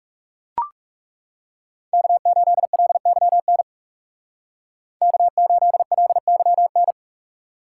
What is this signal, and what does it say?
Morse code, heard as a keyed tone, about 38 wpm, 700 Hz. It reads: K8LYN K8LYN
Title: Speed Racing - USA 1x3 Call Signs 30wpm